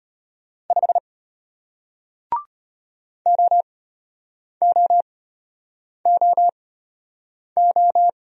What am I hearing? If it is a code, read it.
Morse code, heard as a keyed tone, about 33 wpm, spaced as for 8 wpm, 700 Hz. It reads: 5 OOOO